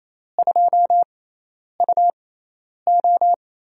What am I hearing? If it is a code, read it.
Morse code, heard as a keyed tone, about 28 wpm, spaced as for 10 wpm, 700 Hz. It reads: 2UO
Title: USA Call Signs - 6x Character Spacing - 28wpm